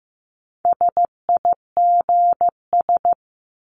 Morse code, heard as a keyed tone, about 15 wpm, 700 Hz. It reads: SIGS
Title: Abbreviations Repeated 3x - 15wpm